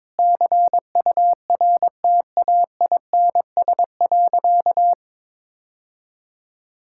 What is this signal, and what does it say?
Morse code, heard as a keyed tone, about 22 wpm, 700 Hz. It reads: CURTAINS.
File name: practice / 22wpm / Easy Sentences with Hints - 22wpm